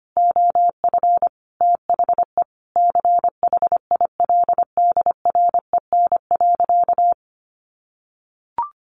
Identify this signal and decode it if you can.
Morse code, heard as a keyed tone, about 25 wpm, 700 Hz. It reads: OF THE CHILDREN.
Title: Sentences from Top 1200 Words 25wpm